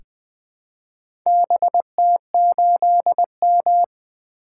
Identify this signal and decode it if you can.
Morse code, heard as a keyed tone, about 20 wpm, 700 Hz. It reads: BT8M
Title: Four Letter-Number Combinations - 20wpm